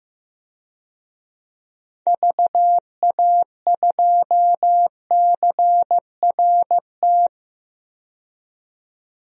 Morse code, heard as a keyed tone, about 15 wpm, 700 Hz. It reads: VA2CRT
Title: DXCC-100 Call Signs - 6 Characters - 15wpm